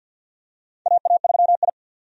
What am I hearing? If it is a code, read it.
Morse code, heard as a keyed tone, about 51 wpm, 700 Hz. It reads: AA3I
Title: USA Call Signs - 4 Characters - 50wpm